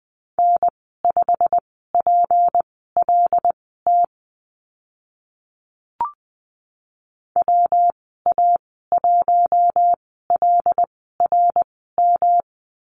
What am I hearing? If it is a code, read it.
Morse code, heard as a keyed tone, about 20 wpm, spaced as for 14 wpm, 700 Hz. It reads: N5PLT WA1LRM
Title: 161 Lesson 26g - Letter g - Call Signs 20wpm